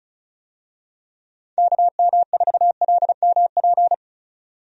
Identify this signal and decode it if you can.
Morse code, heard as a keyed tone, about 35 wpm, 700 Hz. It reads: KM4LMP
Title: USA Call Signs - 6 Characters - 35wpm